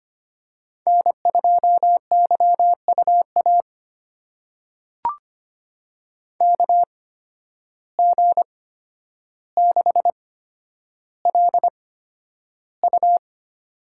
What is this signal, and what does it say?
Morse code, heard as a keyed tone, about 25 wpm, spaced as for 11 wpm, 700 Hz. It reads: N2YUA KG6LU
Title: USA Call Signs - 8x Character Spacing - 25wpm